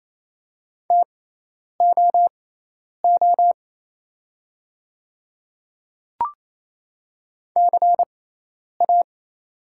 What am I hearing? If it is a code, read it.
Morse code, heard as a keyed tone, about 28 wpm, spaced as for 10 wpm, 700 Hz. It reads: TOO CA
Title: 6x Character Spacing - Top 500 Words - 28wpm